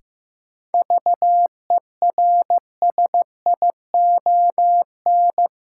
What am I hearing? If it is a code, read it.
Morse code, heard as a keyed tone, about 15 wpm, 700 Hz. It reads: VERSION
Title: Sentences from Top 1400 Words - New Word Drill Alt - 15wpm